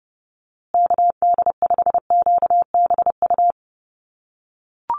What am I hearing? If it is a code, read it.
Morse code, heard as a keyed tone, about 30 wpm, 700 Hz. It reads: KD5QBU
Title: USA Call Signs Repeated 3x 30wpm